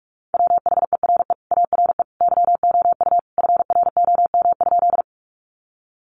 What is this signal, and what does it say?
Morse code, heard as a keyed tone, about 44 wpm, 700 Hz. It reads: WHERE ARE YOU FROM?
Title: Sentences from Top 200 Words 45wpm